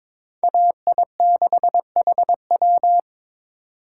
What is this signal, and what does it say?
Morse code, heard as a keyed tone, about 26 wpm, 700 Hz. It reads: AI6HW
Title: Speed Racing - USA 2x2 Call Signs 22wpm